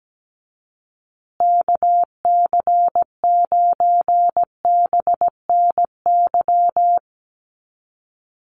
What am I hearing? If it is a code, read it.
Morse code, heard as a keyed tone, about 17 wpm, 700 Hz. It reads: KC9BNY
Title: USA Call Signs Repeated 2x 17wpm